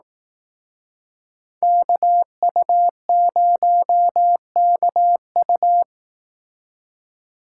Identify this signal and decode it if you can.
Morse code, heard as a keyed tone, about 18 wpm, 700 Hz. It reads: KU0KU
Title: Speed Racing - USA 2x2 Call Signs 15wpm